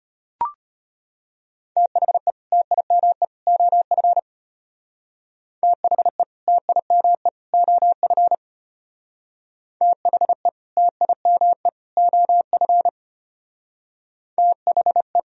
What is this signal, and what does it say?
Morse code, heard as a keyed tone, about 33 wpm, 700 Hz. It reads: THE TIME OF THE TIME OF THE TIME OF THE